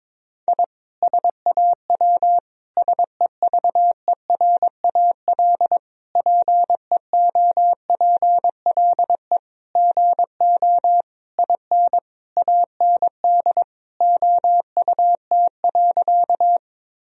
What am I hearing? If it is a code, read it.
Morse code, heard as a keyed tone, about 22 wpm, 700 Hz. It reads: I SAW SEVERAL PEOPLE GO IN AND OUT.
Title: Sentences from Top 1000 Words - Review Part 2 - 22wpm